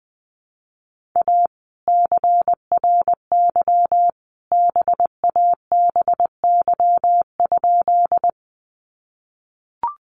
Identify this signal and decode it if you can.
Morse code, heard as a keyed tone, about 20 wpm, 700 Hz. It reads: A CRY BABY?
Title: Sentences from Top 1300 Words 20wpm